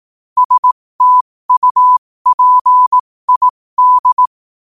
Morse code, heard as a keyed tone, about 18 wpm, spaced as for 15 wpm, 1000 Hz. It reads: STUPID
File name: qotdfiles / 15wpm